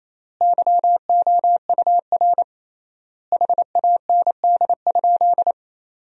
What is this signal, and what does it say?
Morse code, heard as a keyed tone, about 28 wpm, 700 Hz. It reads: YOUR HAND?
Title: 3x Word-Spacing - Sentences from Top 200 Words - 28wpm